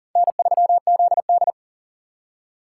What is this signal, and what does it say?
Morse code, heard as a keyed tone, about 40 wpm, 700 Hz. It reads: N3ZD
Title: USA Call Signs - 4 Characters - 3x Character Spacing - 40wpm